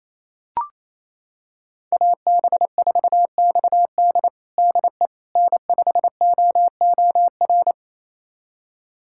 Morse code, heard as a keyed tone, about 28 wpm, 700 Hz. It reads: AB4XD DE N5OOR
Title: USA Callsign DE Callsign - 5 Characters - 28wpm